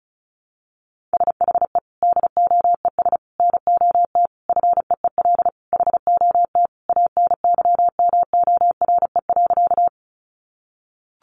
Morse code, heard as a keyed tone, about 34 wpm, 700 Hz. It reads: SHE DOES NOT FEEL HOT ANYMORE.
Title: Easy Phrases 35wpm